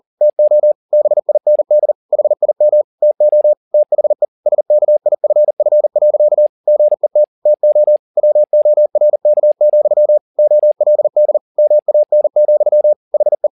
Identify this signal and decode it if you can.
Morse code, heard as a keyed tone, about 40 wpm, 600 Hz. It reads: TO BIND HIM TO THE SKIFF. GET TO WORK, OLD MAN, HE
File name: the old man and the sea 2 40wpm